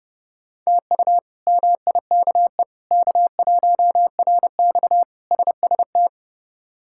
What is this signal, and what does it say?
Morse code, heard as a keyed tone, about 30 wpm, 700 Hz. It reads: TU MIKE K1RX SST
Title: K1USN SST QSO Practice - 30wpm